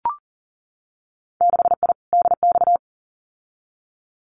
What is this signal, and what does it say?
Morse code, heard as a keyed tone, about 40 wpm, 700 Hz. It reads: 6I DX
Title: Winter Field Day Exchanges Encore 40wpm